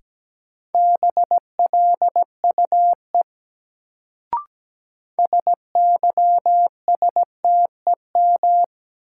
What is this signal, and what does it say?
Morse code, heard as a keyed tone, about 17 wpm, 700 Hz. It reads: BLUE SYSTEM
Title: Sentences from Top 1000 Words - Review Part 3 - New Word Drill - 17wpm